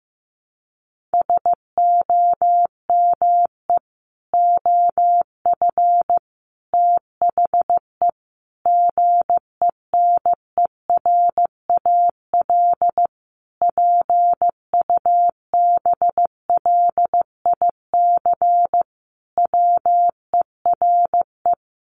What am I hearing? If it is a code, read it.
Morse code, heard as a keyed tone, about 15 wpm, 700 Hz. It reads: SOME OF THE GENERAL PUBLIC WERE
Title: Sentences from Top 800 Words 15wpm